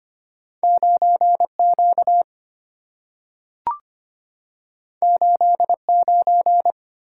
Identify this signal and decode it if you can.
Morse code, heard as a keyed tone, about 25 wpm, 690 Hz. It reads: 9Q 89
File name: Two Letter-Number Combinations - 25wpm